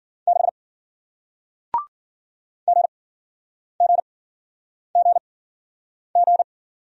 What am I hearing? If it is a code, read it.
Morse code, heard as a keyed tone, about 52 wpm, spaced as for 9 wpm, 700 Hz. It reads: 6 GGGG